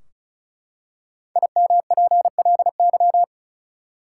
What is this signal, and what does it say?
Morse code, heard as a keyed tone, about 36 wpm, 700 Hz. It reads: IMPLY
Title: Five Letter Words - 35wpm